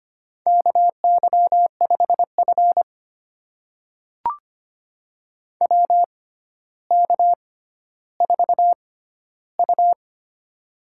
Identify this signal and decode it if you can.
Morse code, heard as a keyed tone, about 25 wpm, spaced as for 13 wpm, 700 Hz. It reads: KY5F WK4U
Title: USA Call Signs - 4 Characters - 6x Character Spacing - 25wpm